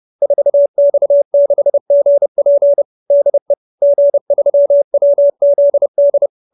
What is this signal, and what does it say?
Morse code, heard as a keyed tone, about 30 wpm, 570 Hz. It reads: 4X6GP DE G3WZD
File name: QSO310 30